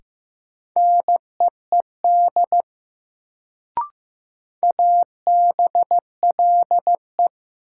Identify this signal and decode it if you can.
Morse code, heard as a keyed tone, about 15 wpm, 700 Hz. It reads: NEED ABLE
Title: Sets of 2 Words - New Word Drill - 15wpm